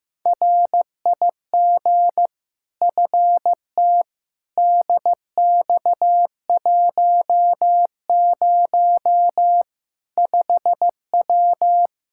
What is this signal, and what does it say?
Morse code, heard as a keyed tone, about 15 wpm, 700 Hz. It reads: RIG FT DX10 5W
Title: Practice QSO - Exchange 2 - Info - 15wpm